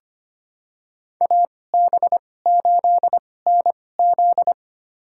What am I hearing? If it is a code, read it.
Morse code, heard as a keyed tone, about 25 wpm, spaced as for 18 wpm, 700 Hz. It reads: AB8NZ
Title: USA Call Signs - 5 Characters - 2x Character Spacing - 25wpm